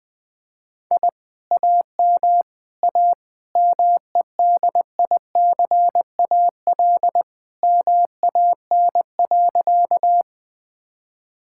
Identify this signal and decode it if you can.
Morse code, heard as a keyed tone, about 19 wpm, 700 Hz. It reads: I AM A MEDICAL MAN.
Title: Sentences from Top 1000 Words - Review Part 5 - 20wpm